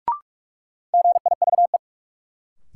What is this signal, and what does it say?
Morse code, heard as a keyed tone, about 45 wpm, 700 Hz. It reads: GIVE
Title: Sets of 2 Words - New Word Drill - 45wpm